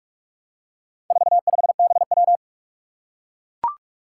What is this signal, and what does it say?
Morse code, heard as a keyed tone, about 44 wpm, 700 Hz. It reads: 45BW
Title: Four Challenging Letter-Number Combinations 45wpm